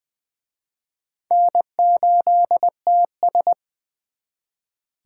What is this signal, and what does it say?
Morse code, heard as a keyed tone, about 24 wpm, 700 Hz. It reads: N8TS
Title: Speed Racing - USA 1x2 Call Signs 20wpm